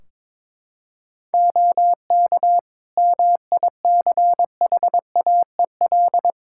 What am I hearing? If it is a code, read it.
Morse code, heard as a keyed tone, about 22 wpm, 700 Hz. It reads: OK MICHAEL
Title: K1USN SST QSO Practice 22wpm